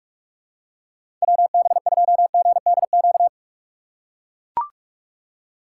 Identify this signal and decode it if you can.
Morse code, heard as a keyed tone, about 41 wpm, 700 Hz. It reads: WB2GDQ
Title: USA Call Signs - 6 Characters - 2x Character Spacing - 45wpm